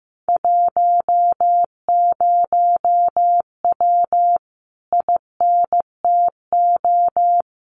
Morse code, heard as a keyed tone, about 15 wpm, 700 Hz. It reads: 10W INTO
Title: Common QSO Phrases Encore 15wpm